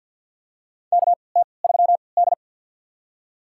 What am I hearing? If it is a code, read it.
Morse code, heard as a keyed tone, about 53 wpm, spaced as for 29 wpm, 700 Hz. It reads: KT3D
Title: USA Call Signs - 4 Characters - 3x Character Spacing - 50wpm